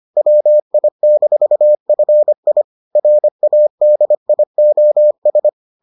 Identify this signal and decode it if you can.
Morse code, heard as a keyed tone, about 25 wpm, 600 Hz. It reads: WI-FI RADIOS